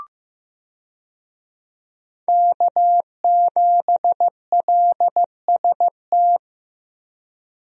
Morse code, heard as a keyed tone, about 15 wpm, 700 Hz. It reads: K7LST
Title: USA Call Signs - 5 Characters - Repeated 2x - 15wpm